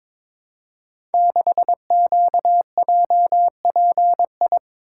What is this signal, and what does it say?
Morse code, heard as a keyed tone, about 22 wpm, 700 Hz. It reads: 6QJPI
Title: Code Groups - 5 Letter:Numbers - 22wpm